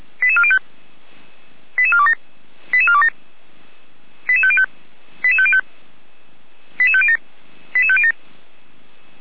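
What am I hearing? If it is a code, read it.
Five-tone selective call (ZVEI), heard as quick runs of five tones, tones of about 70 ms. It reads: 88405 88427 88427 88505 88505 88507 88507
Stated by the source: ZVEI five-tone selective call received by SDR, from the Feuerwehr Orlamünde (fire brigade) audio archive.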